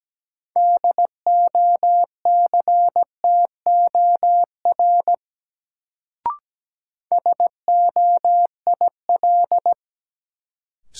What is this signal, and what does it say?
Morse code, heard as a keyed tone, about 17 wpm, 700 Hz. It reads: DOCTOR SOIL
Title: Sentences from Top 1500 Words - New Word Drill Alt - 17wpm